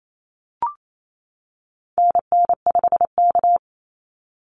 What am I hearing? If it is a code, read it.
Morse code, heard as a keyed tone, about 28 wpm, 700 Hz. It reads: NN5K
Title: NA SOTA Activator Callsigns 28wpm